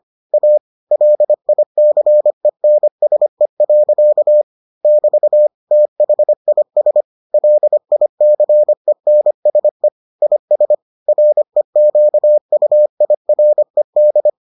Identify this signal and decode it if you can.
Morse code, heard as a keyed tone, about 25 wpm, 600 Hz. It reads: A LICENSE. <BT> THIS LICENSE IS REQUIRED